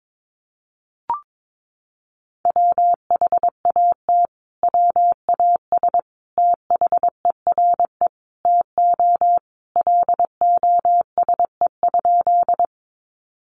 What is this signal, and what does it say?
Morse code, heard as a keyed tone, about 22 wpm, 700 Hz. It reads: WHAT WAS THERE TO LOSE?
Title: Sentences from Top 1100 Words 22wpm